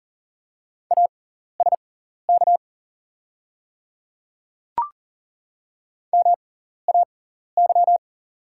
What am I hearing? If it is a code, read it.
Morse code, heard as a keyed tone, about 40 wpm, spaced as for 14 wpm, 700 Hz. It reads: ASK MAY
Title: 6x Character Spacing - Top 300 Words - 40wpm